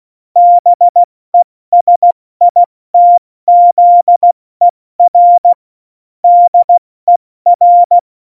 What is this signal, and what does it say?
Morse code, heard as a keyed tone, about 16 wpm, 700 Hz. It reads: BESITZER DER
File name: Morsekurs / Witze / Witz 14 Juli 2020, 70 F80